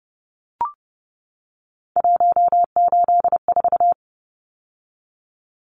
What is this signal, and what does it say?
Morse code, heard as a keyed tone, about 29 wpm, 700 Hz. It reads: 184
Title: Numbers - 3-Digit 30wpm